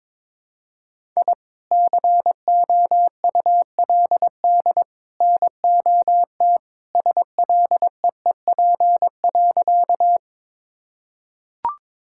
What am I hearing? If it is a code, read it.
Morse code, heard as a keyed tone, about 22 wpm, 700 Hz. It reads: I COULD NOT SLEEP.
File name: practice / 22wpm / Sentences with 3-Word Phrases - 22wpm